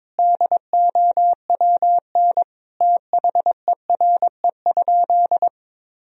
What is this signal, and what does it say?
Morse code, heard as a keyed tone, about 22 wpm, 700 Hz. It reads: DOWN THERE?
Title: Sentences from Top 1000 Words - Review Part 4 - 22wpm